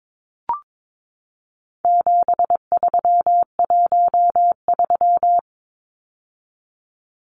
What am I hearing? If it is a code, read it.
Morse code, heard as a keyed tone, about 22 wpm, 700 Hz. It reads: 7313
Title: Numbers - 4-Digit 22wpm